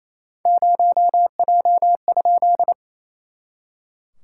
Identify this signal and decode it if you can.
Morse code, heard as a keyed tone, about 28 wpm, 700 Hz. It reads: 0J?
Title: SOTA QSO Practice 28wpm